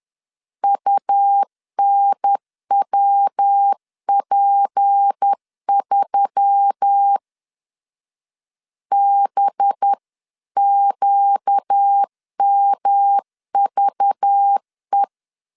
Morse code, heard as a keyed tone, about 10 wpm, 800 Hz. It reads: UNWP3 BQMVE